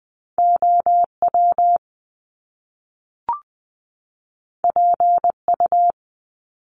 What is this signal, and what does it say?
Morse code, heard as a keyed tone, about 20 wpm, 700 Hz. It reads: OW PU
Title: Top Two Letter Combinations in Words 20wpm